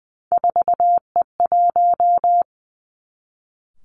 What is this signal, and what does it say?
Morse code, heard as a keyed tone, about 20 wpm, 700 Hz. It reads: 4E1
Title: Three Challenging Letter-Number Combinations 20wpm